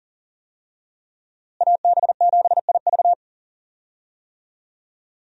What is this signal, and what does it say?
Morse code, heard as a keyed tone, about 40 wpm, 700 Hz. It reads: AB7IV
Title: USA Call Signs - 5 Characters - Repeated 3x - 40wpm